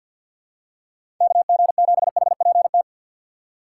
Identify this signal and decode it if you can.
Morse code, heard as a keyed tone, about 52 wpm, 700 Hz. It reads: KG7HPT
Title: USA Call Signs - 3x Character Spacing - 50wpm